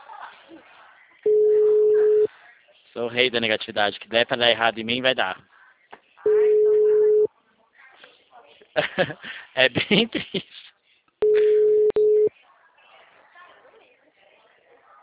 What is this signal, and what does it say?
Ringback tone (Europe, 425 Hz), with speech.
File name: out- -2007- - - .0